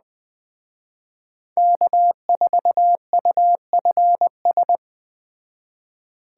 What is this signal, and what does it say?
Morse code, heard as a keyed tone, about 20 wpm, 700 Hz. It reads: K4UFS
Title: Speed Racing - USA 1x3 Call Signs 17wpm